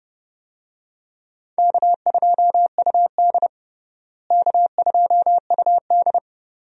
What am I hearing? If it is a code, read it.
Morse code, heard as a keyed tone, about 30 wpm, 700 Hz. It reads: K2UD K2UD
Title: USA Call Signs - 4 Characters - Repeated 2x - 30wpm